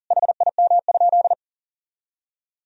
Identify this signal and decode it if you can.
Morse code, heard as a keyed tone, about 40 wpm, 700 Hz. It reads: HIM?
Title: Sentences from Top 100 Words and Abbreviations 40wpm